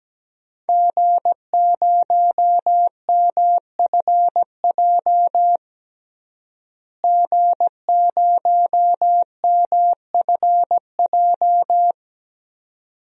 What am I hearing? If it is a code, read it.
Morse code, heard as a keyed tone, about 17 wpm, 700 Hz. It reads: G0MFJ G0MFJ
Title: DXCC-100 Call Signs - 5 Characters - Repeated 2x - 17wpm